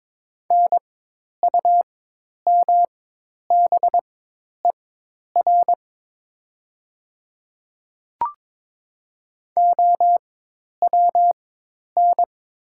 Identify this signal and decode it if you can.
Morse code, heard as a keyed tone, about 22 wpm, spaced as for 10 wpm, 700 Hz. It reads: NUMBER OWN
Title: 4x Character Spacing - Top 500 Words - 22wpm